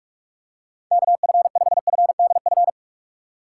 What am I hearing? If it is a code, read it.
Morse code, heard as a keyed tone, about 41 wpm, 700 Hz. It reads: KF5FDF
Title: USA Call Signs - 6 Characters - 2x Character Spacing - 45wpm